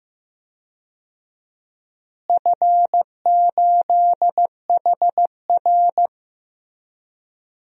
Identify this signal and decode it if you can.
Morse code, heard as a keyed tone, about 16 wpm, 700 Hz. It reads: F8HR
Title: DXCC-100 Call Signs - 4 Characters - Repeated 3x - 15wpm